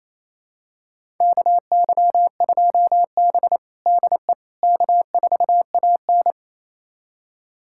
Morse code, heard as a keyed tone, about 28 wpm, 700 Hz. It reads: KY2B DE K4AN
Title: USA Callsign DE Callsign - 4 Characters - 28wpm